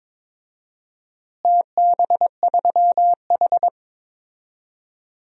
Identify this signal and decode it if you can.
Morse code, heard as a keyed tone, about 22 wpm, 700 Hz. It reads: TB3H
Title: DXCC-100 Call Signs - 4 Characters - Repeated 3x - 22wpm